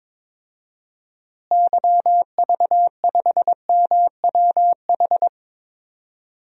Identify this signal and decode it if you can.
Morse code, heard as a keyed tone, about 22 wpm, 700 Hz. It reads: YV5MWH
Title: DXCC-100 Call Signs - 6 Characters - Repeated 2x - 22wpm